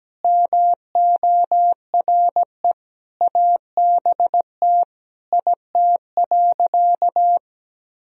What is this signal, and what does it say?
Morse code, heard as a keyed tone, about 17 wpm, 700 Hz. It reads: MORE ABT IT.
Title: Sentences from Top 200 Words and Abbreviations 17wpm